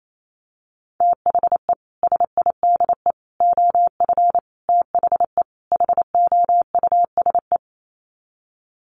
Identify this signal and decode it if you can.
Morse code, heard as a keyed tone, about 28 wpm, 700 Hz. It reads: THE SIDE OF THE HOUSE
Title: Sets of 5 Words Encore2 28wpm